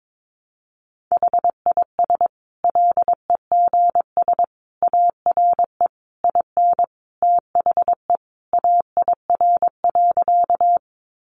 Morse code, heard as a keyed tone, about 22 wpm, 700 Hz. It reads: HIS LEGS ARE IN THE AIR.
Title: Additional Easy Phrases 22wpm